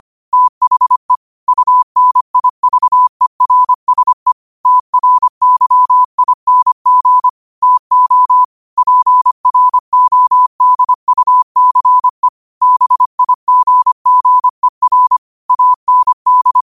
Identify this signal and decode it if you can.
Morse code, heard as a keyed tone, about 25 wpm, 1000 Hz. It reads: THE UNIVERSE TRYING TO PRODUCE BIGGER AND
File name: qotdfiles / 25wpm